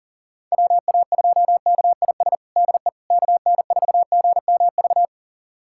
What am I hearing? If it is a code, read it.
Morse code, heard as a keyed tone, about 40 wpm, 700 Hz. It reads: WA2KIS DE KN4GMV
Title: USA Callsign DE Callsign 40wpm